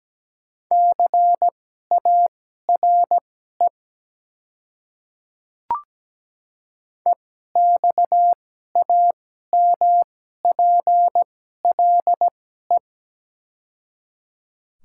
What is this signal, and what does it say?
Morse code, heard as a keyed tone, about 17 wpm, spaced as for 12 wpm, 700 Hz. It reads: CARE EXAMPLE
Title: 2x Character Spacing - Top 300 Words - 17wpm